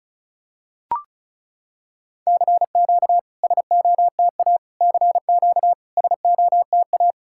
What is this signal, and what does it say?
Morse code, heard as a keyed tone, about 36 wpm, 690 Hz. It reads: CQ SOTA CQ SOTA
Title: SOTA QSO Practice 35wpm